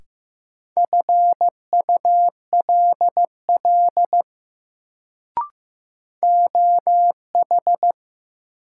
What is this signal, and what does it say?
Morse code, heard as a keyed tone, about 15 wpm, 700 Hz. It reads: FULL OH
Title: Sentences from Top 500 Words - New Word Drill - 15wpm